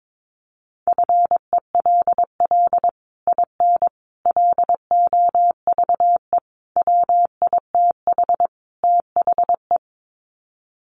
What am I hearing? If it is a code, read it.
Morse code, heard as a keyed tone, about 22 wpm, 700 Hz. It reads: FELL IN LOVE WITH THE
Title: Sets of 5 Words Encore2 22wpm